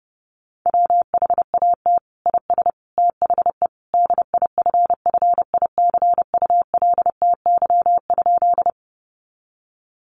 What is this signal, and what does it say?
Morse code, heard as a keyed tone, about 29 wpm, 700 Hz. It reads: WHAT IS THE DIFFICULTY?
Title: Sentences from Top 1500 Words 30wpm